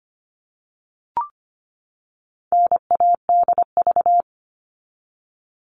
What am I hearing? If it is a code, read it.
Morse code, heard as a keyed tone, about 25 wpm, 700 Hz. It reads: NADV
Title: All Cut Numbers - 4-Digit 25wpm